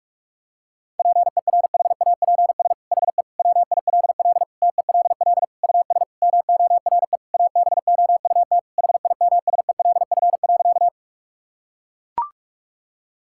Morse code, heard as a keyed tone, about 41 wpm, 700 Hz. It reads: PERHAPS HE WILL TELL US MORE ABOUT HIMSELF.
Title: Sentences from Top 1600 Words 45wpm